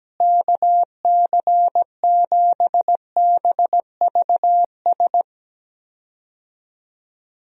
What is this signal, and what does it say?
Morse code, heard as a keyed tone, about 17 wpm, 700 Hz. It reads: KC7BVS
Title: USA Call Signs - 6 Characters - Repeated 2x - 17wpm